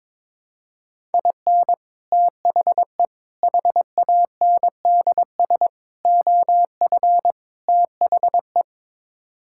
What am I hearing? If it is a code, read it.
Morse code, heard as a keyed tone, about 21 wpm, 700 Hz. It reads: IN THE HANDS OF THE
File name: practice / 22wpm / Build A Phrase - 5-Word Phrases - 22wpm